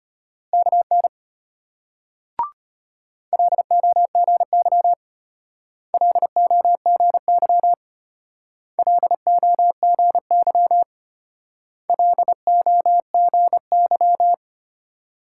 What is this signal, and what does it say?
Morse code, heard as a keyed tone, about 32 wpm, 700 Hz. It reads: KN LOGY LOGY LOGY LOGY